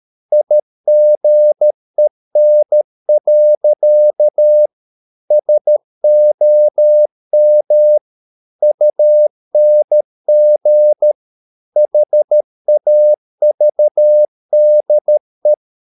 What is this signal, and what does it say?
Morse code, heard as a keyed tone, about 13 wpm, 600 Hz. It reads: IGEN. SOM UNG HAVDE
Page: Ferie 13 wpm